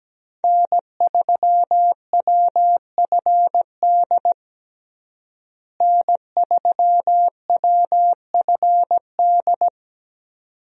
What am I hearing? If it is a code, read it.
Morse code, heard as a keyed tone, about 17 wpm, 700 Hz. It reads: N3WFD N3WFD
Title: USA Call Signs - 5 Characters - Repeated 2x - 17wpm